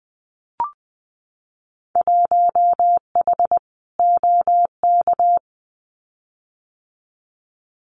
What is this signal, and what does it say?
Morse code, heard as a keyed tone, about 20 wpm, 700 Hz. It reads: 1H OK
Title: Winter Field Day Exchanges 20wpm